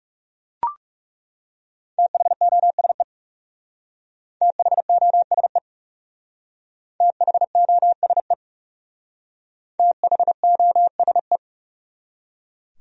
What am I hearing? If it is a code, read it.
Morse code, heard as a keyed tone, about 38 wpm, 700 Hz. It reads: THOSE THOSE THOSE THOSE